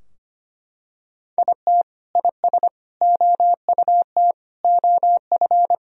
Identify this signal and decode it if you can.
Morse code, heard as a keyed tone, about 25 wpm, 700 Hz. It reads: IT IS OUT OF
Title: Sentences from Top 1000 Words - Review Part 4 - 25wpm